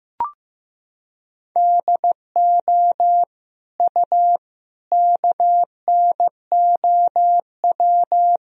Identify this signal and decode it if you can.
Morse code, heard as a keyed tone, about 16 wpm, 700 Hz. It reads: DO U KNOW
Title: Sentences from Top 200 Words and Abbreviations 15wpm